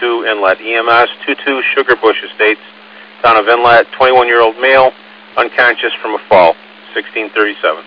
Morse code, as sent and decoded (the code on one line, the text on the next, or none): none